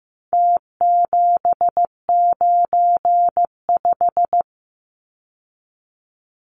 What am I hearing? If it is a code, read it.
Morse code, heard as a keyed tone, about 15 wpm, 700 Hz. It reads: T795
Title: Numbers - Contest Style - 4-Digit 15wpm